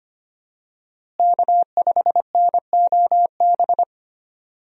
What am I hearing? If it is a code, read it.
Morse code, heard as a keyed tone, about 25 wpm, 700 Hz. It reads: K5NOB
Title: USA Call Signs - 5 Characters - Repeated 3x - 25wpm